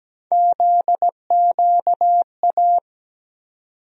Morse code, heard as a keyed tone, about 17 wpm, 700 Hz. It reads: ZQA
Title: USA Call Sign Postfixes - 3 Letters 17wpm